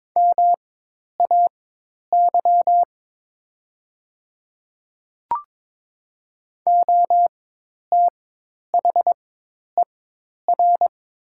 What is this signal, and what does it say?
Morse code, heard as a keyed tone, about 22 wpm, spaced as for 10 wpm, 700 Hz. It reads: MAY OTHER
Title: 4x Character Spacing - Top 200 Words - 22wpm